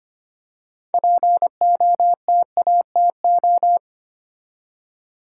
Morse code, heard as a keyed tone, about 25 wpm, 700 Hz. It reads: POTATO
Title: Six Letter Words - 25wpm